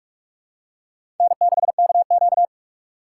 Morse code, heard as a keyed tone, about 43 wpm, 700 Hz. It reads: N6KQ
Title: USA Call Signs - 4 Characters - 2x Character Spacing - 45wpm